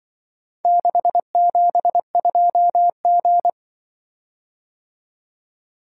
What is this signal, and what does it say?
Morse code, heard as a keyed tone, about 24 wpm, 700 Hz. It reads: 672G
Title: Speed-Racing - Four Letter-Number Combinations - 20wpm